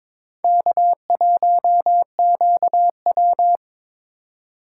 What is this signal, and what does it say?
Morse code, heard as a keyed tone, about 26 wpm, 700 Hz. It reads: K1QW
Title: Speed Racing - USA 1x2 Call Signs 22wpm